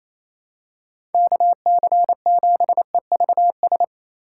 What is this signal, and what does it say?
Morse code, heard as a keyed tone, about 28 wpm, 700 Hz. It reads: KC7EVS
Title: USA Call Signs - 2x Character Spacing - 28wpm